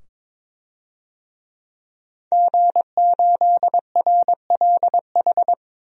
Morse code, heard as a keyed tone, about 21 wpm, 700 Hz. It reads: G8RLH
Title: DXCC-100 Call Signs - 5 Characters - 22wpm